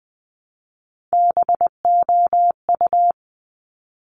Morse code, heard as a keyed tone, about 20 wpm, 700 Hz. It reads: BOU
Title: Top Three Letter Combinations in Words 20wpm